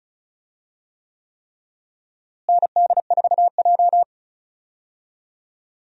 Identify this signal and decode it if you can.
Morse code, heard as a keyed tone, about 36 wpm, 700 Hz. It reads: ND4J
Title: USA Call Signs - 4 Characters - Repeated 3x - 35wpm